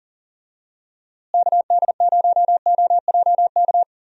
Morse code, heard as a keyed tone, about 40 wpm, 700 Hz. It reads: KD0OJK
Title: USA Call Signs - 6 Characters - Repeated 3x - 40wpm